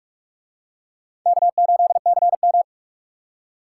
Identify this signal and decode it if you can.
Morse code, heard as a keyed tone, about 43 wpm, 700 Hz. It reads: K8CM